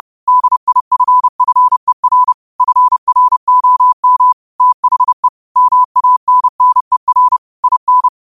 Morse code, heard as a keyed tone, about 30 wpm, 1000 Hz. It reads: DIFFER FROM THE MANNER IN
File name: qotdfiles / 30wpm